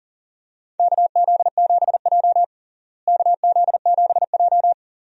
Morse code, heard as a keyed tone, about 40 wpm, 700 Hz. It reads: KZ7J KZ7J
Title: USA Call Signs - 4 Characters - Repeated 3x - 40wpm